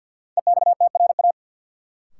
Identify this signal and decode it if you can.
Morse code, heard as a keyed tone, about 50 wpm, 700 Hz. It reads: EXTRA
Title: Sentences from Top 1000 Words - New Word Drill - 50wpm